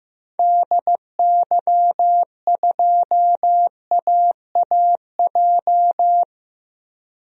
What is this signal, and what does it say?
Morse code, heard as a keyed tone, about 15 wpm, 700 Hz. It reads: DY2AAJ
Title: DXCC-100 Call Signs - 6 Characters - 15wpm